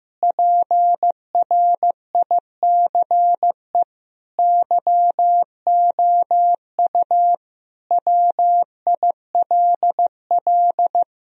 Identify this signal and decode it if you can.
Morse code, heard as a keyed tone, about 15 wpm, 700 Hz. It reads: PRICE YOU WILL
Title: Sentences from Top 1000 Words - Review Part 4 - 15wpm